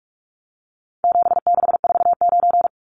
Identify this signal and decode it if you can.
Morse code, heard as a keyed tone, about 44 wpm, 700 Hz. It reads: 7649
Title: Numbers - 4-Digit 45wpm